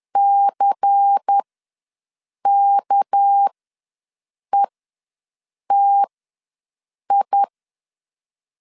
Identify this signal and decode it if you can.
Morse code, heard as a keyed tone, about 10 wpm, spaced as for 6 wpm, 800 Hz. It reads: CKETI
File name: LEZIONI / 6 PKX0 / 6 PKX ws 10wpm 3706